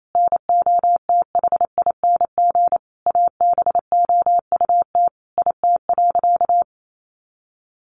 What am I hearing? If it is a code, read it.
Morse code, heard as a keyed tone, about 28 wpm, 700 Hz. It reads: NOTHING ABOUT IT.
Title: Sentences from Top 500 Words 28wpm